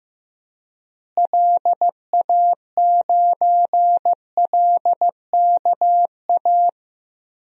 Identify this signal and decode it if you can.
Morse code, heard as a keyed tone, about 15 wpm, 700 Hz. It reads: LA9LKA
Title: DXCC-100 Call Signs - 6 Characters - Repeated 2x - 15wpm